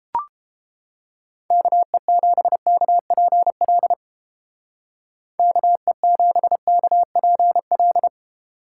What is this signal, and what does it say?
Morse code, heard as a keyed tone, about 33 wpm, 700 Hz. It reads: KE7KPL KE7KPL